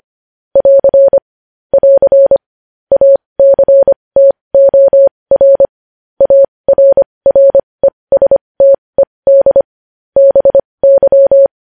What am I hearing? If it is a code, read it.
Morse code, heard as a keyed tone, about 25 wpm, spaced as for 20 wpm, 550 Hz. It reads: <AR> <AR> ACTOR ARRESTED BY
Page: news-headlines-in-morse-code-at-20-wpm